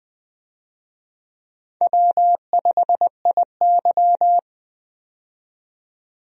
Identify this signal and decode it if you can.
Morse code, heard as a keyed tone, about 20 wpm, 700 Hz. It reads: W5IY
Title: USA Call Signs - 4 Characters - 20wpm